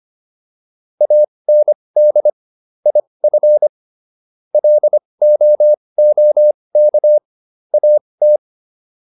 Morse code, heard as a keyed tone, about 25 wpm, spaced as for 20 wpm, 600 Hz. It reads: AND IF LOOK AT